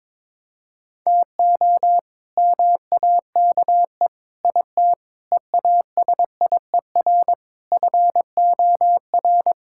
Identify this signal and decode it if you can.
Morse code, heard as a keyed tone, about 22 wpm, 700 Hz. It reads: TO MAKE IT EASIER FOR
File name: practice / 22wpm / Sets of 7 Words - 22wpm